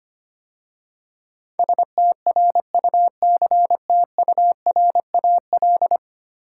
Morse code, heard as a keyed tone, about 25 wpm, 690 Hz. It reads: STRUCTURAL
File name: 10 Letter Words - First 3 Letters Given - 25wpm